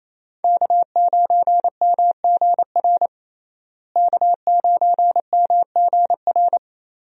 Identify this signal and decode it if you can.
Morse code, heard as a keyed tone, about 28 wpm, 700 Hz. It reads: K9MGR K9MGR
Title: USA Call Signs - 5 Characters - Repeated 3x - 28wpm